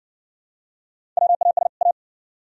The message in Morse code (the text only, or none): WAS A